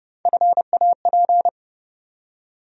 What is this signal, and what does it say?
Morse code, heard as a keyed tone, about 30 wpm, 700 Hz. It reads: FAP
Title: USA Call Sign Postfixes - 3 Letters 30wpm